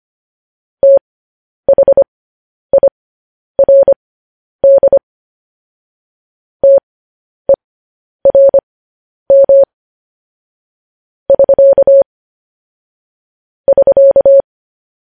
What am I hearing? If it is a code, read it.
Morse code, heard as a keyed tone, about 25 wpm, spaced as for 10 wpm, 550 Hz. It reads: THIRD TERM <SK> <SK>
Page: news-headlines-in-morse-code-at-10-wpm